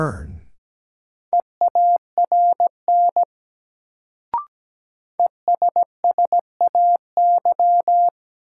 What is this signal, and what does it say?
Morse code, heard as a keyed tone, about 17 wpm, 700 Hz. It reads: EARN ESSAY
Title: Sentences from Top 1500 Words - New Word Drill - 17wpm